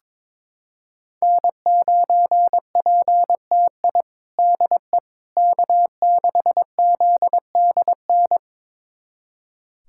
Morse code, heard as a keyed tone, about 22 wpm, 700 Hz. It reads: N9PTI DE K6ZDN
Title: USA Callsign DE Callsign - 5 Characters - 22wpm